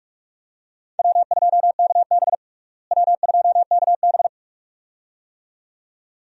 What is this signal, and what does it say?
Morse code, heard as a keyed tone, about 43 wpm, 700 Hz. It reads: W2KB W2KB
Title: USA Call Signs - 4 Characters - Repeated 2x - 45wpm